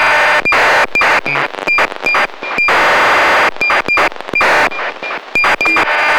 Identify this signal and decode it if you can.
ACARS aircraft data.
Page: ACARS � ������ � 18.00 �� 131,725